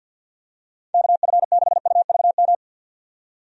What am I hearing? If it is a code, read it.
Morse code, heard as a keyed tone, about 49 wpm, 700 Hz. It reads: KF6UVM
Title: USA Call Signs - 6 Characters - 50wpm